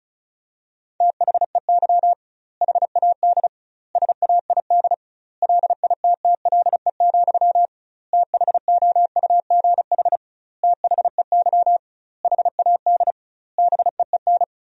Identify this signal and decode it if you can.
Morse code, heard as a keyed tone, about 36 wpm, 700 Hz. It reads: THEY HAD SAID LITTLE, THOUGH THEY HAD BEEN
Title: 2x Word-Spacing - Sentences from Top 400 Words - 35wpm